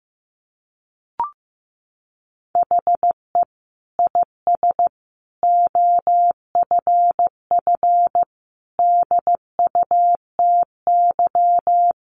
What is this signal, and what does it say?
Morse code, heard as a keyed tone, about 15 wpm, 700 Hz. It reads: HE IS OFF DUTY
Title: Sentences from Top 1500 Words 15wpm